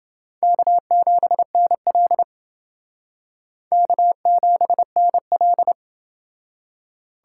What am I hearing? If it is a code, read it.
Morse code, heard as a keyed tone, about 28 wpm, 700 Hz. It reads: K7NL K7NL